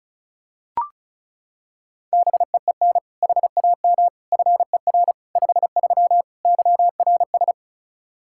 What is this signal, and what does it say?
Morse code, heard as a keyed tone, about 34 wpm, 700 Hz. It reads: BEEN HAM FER 53 YRS
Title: Practice QSO - Exchange 3 - More Info - 35wpm